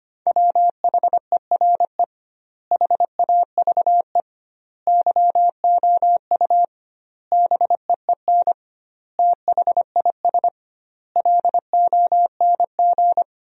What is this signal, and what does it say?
Morse code, heard as a keyed tone, about 25 wpm, 700 Hz. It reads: WHERE HAVE YOU BEEN THIS LONG
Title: 2x Word-Spacing - Sentences from Top 300 Words - 25wpm